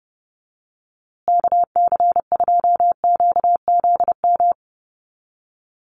Morse code, heard as a keyed tone, about 30 wpm, 700 Hz. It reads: KC2QZM
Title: USA Call Signs Repeated 2x 30wpm